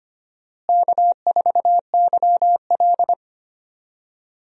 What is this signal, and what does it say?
Morse code, heard as a keyed tone, about 25 wpm, 700 Hz. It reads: K4YL
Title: USA Call Signs - 4 Characters - Repeated 2x - 25wpm